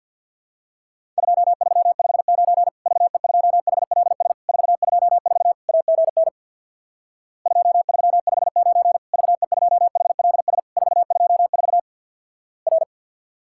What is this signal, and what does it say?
Morse code, heard as a keyed tone, about 51 wpm, 700 Hz. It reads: 2359 VE2HLS 424 AGN 2359 VE2HLS 424 R
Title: Worked All Europe DX Contest - QTC Exchange with Repeats - 50wpm